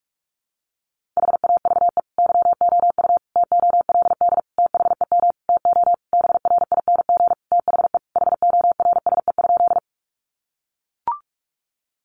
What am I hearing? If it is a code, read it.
Morse code, heard as a keyed tone, about 44 wpm, 700 Hz. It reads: HAVE YOU TOLD THEM TO BRING THE HORSE?
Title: Sentences from Top 500 Words 45wpm